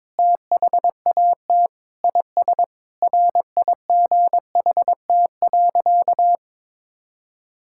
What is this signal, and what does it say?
Morse code, heard as a keyed tone, about 22 wpm, 700 Hz. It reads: THAT IS RIGHT.
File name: practice / 22wpm / Sentences with 4-Word Phrases - 22wpm